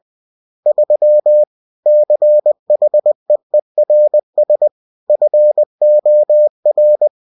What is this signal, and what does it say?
Morse code, heard as a keyed tone, about 20 wpm, 600 Hz. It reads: 3 CHEERS FOR
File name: cw-books / winnie-the-pooh / winnie-20wpm-page-0103